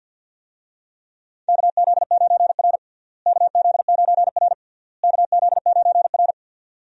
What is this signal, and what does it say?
Morse code, heard as a keyed tone, about 53 wpm, 700 Hz. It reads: KZ9R KZ9R KZ9R
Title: USA Call Signs - 4 Characters - Repeated 3x - 50wpm